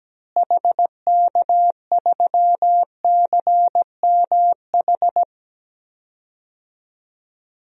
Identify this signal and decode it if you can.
Morse code, heard as a keyed tone, about 17 wpm, 700 Hz. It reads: HK3CMH
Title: DXCC-100 Call Signs - 6 Characters - Repeated 3x - 17wpm